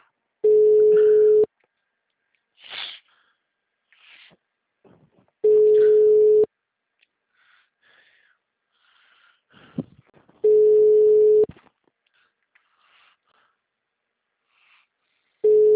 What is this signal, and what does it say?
Ringback tone (Europe, 425 Hz).